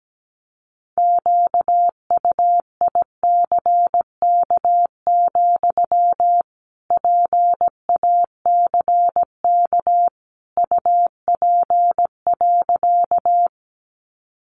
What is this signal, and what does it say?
Morse code, heard as a keyed tone, about 17 wpm, 700 Hz. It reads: QUICK, PACK UP.
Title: Sentences from Top 1300 Words 17wpm